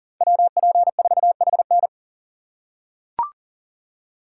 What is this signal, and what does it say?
Morse code, heard as a keyed tone, about 40 wpm, 700 Hz. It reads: WP4HN
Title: USA Call Signs - 5 Characters - 2x Character Spacing - 40wpm